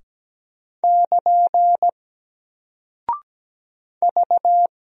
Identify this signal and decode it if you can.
Morse code, heard as a keyed tone, about 17 wpm, 700 Hz. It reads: <KN> V
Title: Single Character (Letter:Number:Punctuation:Prosign) Alt 17wpm